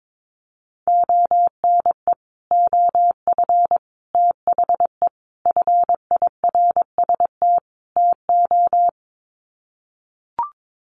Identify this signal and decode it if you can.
Morse code, heard as a keyed tone, about 22 wpm, 700 Hz. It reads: ONE OF THE FIRST TO
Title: Sets of 5 Words Encore 22wpm